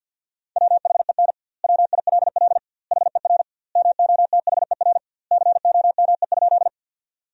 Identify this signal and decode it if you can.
Morse code, heard as a keyed tone, about 50 wpm, 700 Hz. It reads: WHEN WILL HER MOTHER COME?
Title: 2x Word-Spacing - Sentences from Top 200 Words - 50wpm